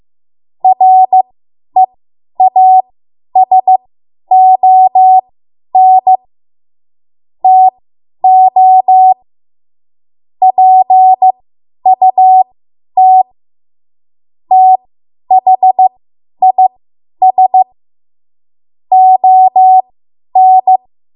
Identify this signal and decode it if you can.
Morse code, heard as a keyed tone, about 15 wpm, spaced as for 10 wpm, 750 Hz. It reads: REASON TO PUT THIS ON
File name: Morse / 10WPM